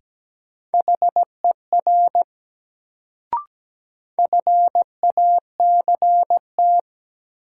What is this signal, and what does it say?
Morse code, heard as a keyed tone, about 17 wpm, 700 Hz. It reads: HER FACT
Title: Sets of 2 Words - New Word Drill - 17wpm